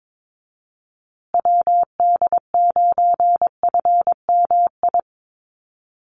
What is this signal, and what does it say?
Morse code, heard as a keyed tone, about 22 wpm, 700 Hz. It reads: WD9FMI
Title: USA Call Signs Repeated 3x 22wpm